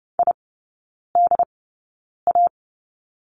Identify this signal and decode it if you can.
Morse code, heard as a keyed tone, about 30 wpm, spaced as for 9 wpm, 700 Hz. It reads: IDA
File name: Lesson 17e - Letter u - Three Characters 30wpm